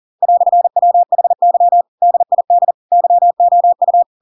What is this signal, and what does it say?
Morse code, heard as a keyed tone, about 40 wpm, 700 Hz. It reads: "WHY DID YOU
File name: Pinocchio / Pinocchio ch8 40 40 ik1hge